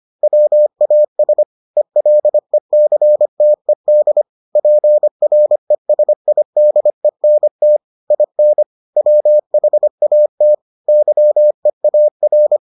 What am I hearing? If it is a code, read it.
Morse code, heard as a keyed tone, about 25 wpm, 600 Hz. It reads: WAS ELECTED PRESIDENT IN WHAT YEAR